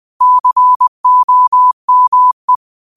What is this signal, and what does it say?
Morse code, heard as a keyed tone, about 20 wpm, 1000 Hz. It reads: COME
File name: qotdfiles / 20wpm